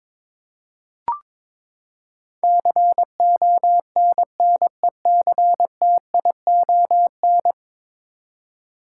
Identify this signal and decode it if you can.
Morse code, heard as a keyed tone, about 22 wpm, 700 Hz. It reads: CONNECTION
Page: morse-code-ninja-practice-22wpm